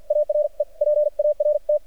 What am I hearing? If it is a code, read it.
Morse code, heard as a keyed tone, about 53 wpm, spaced as for 45 wpm, 600 Hz. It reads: ARE PART
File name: archived / CRSnet